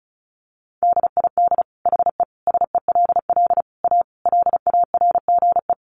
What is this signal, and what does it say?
Morse code, heard as a keyed tone, about 34 wpm, 700 Hz. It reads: DID HE SELL A LARGE
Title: Sentences from Top 1300 Words 35wpm